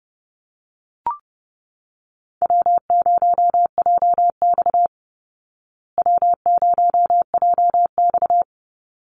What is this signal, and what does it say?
Morse code, heard as a keyed tone, about 29 wpm, 700 Hz. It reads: W0JX W0JX
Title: Call Signs Repeated 3x - 30wpm